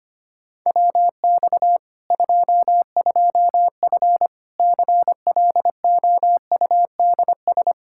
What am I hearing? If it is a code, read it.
Morse code, heard as a keyed tone, about 25 wpm, 700 Hz. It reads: WX 22F CLOUDS
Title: Practice QSO - Exchange 2 - Info - 25wpm